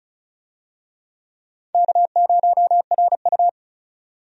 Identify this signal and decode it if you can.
Morse code, heard as a keyed tone, about 36 wpm, 700 Hz. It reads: K0RU
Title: USA Call Signs - 4 Characters - 35wpm